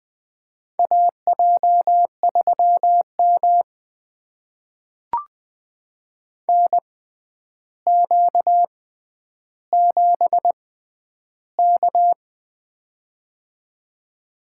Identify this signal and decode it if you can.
Morse code, heard as a keyed tone, about 21 wpm, spaced as for 10 wpm, 700 Hz. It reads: AJ3M NQ7K
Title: USA Call Signs - 4 Characters - 6x Character Spacing - 20wpm